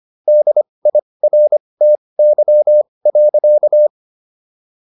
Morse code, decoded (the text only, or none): DIRTY.